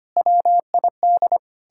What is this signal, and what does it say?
Morse code, heard as a keyed tone, about 31 wpm, 700 Hz. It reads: WID
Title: Speed-Racing Common QSO Phrases - 25wpm